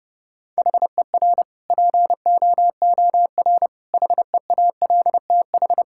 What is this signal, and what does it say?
Morse code, heard as a keyed tone, about 32 wpm, 700 Hz. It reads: HER POOR HEALTH
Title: Sentences from Top 1000 Words - Review Part 6 - 30wpm